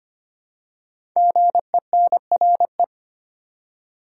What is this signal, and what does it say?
Morse code, heard as a keyed tone, about 25 wpm, 700 Hz. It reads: GENRE
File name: 5 Letter Words - First 3 Letters Given - 25wpm